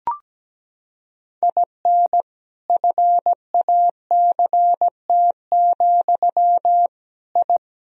Morse code, heard as a keyed tone, about 17 wpm, 700 Hz. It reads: IN FACT, I
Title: Sentences from Top 1000 Words - Review Part 3 - 17wpm